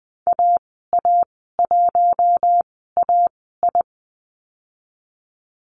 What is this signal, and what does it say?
Morse code, heard as a keyed tone, about 20 wpm, spaced as for 14 wpm, 700 Hz. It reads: AA1AI
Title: 042 Lesson 8g - Number 1 - Call Signs 20wpm